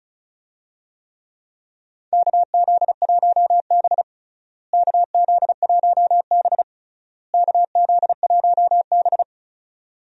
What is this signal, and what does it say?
Morse code, heard as a keyed tone, about 36 wpm, 700 Hz. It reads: KZ1B KZ1B KZ1B
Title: USA Call Signs - 4 Characters - Repeated 3x - 35wpm